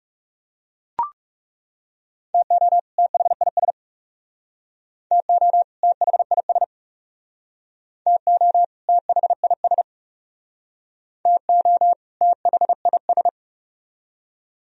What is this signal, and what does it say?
Morse code, heard as a keyed tone, about 41 wpm, 700 Hz. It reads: TO THIS TO THIS TO THIS TO THIS